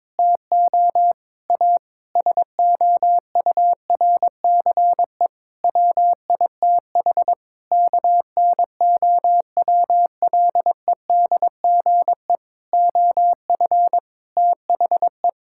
Morse code, heard as a keyed tone, about 22 wpm, 700 Hz. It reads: TO A SOURCE WITH KNOWLEDGE OF THE
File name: Sets of 9 Words - 22wpm